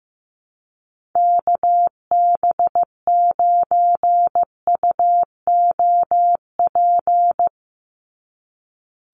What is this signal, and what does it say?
Morse code, heard as a keyed tone, about 15 wpm, 700 Hz. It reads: KB9UOP
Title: USA Call Signs 15wpm